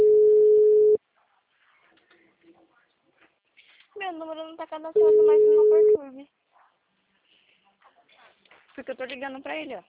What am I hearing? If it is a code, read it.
Ringback tone (Europe, 425 Hz), with speech.